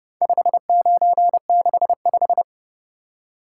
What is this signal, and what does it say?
Morse code, heard as a keyed tone, about 29 wpm, 700 Hz. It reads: 5965